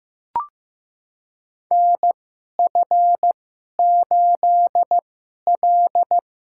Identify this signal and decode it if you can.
Morse code, heard as a keyed tone, about 15 wpm, spaced as for 11 wpm, 700 Hz. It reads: NF8L
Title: USA Call Signs - 4 Characters - 2x Character Spacing - 15wpm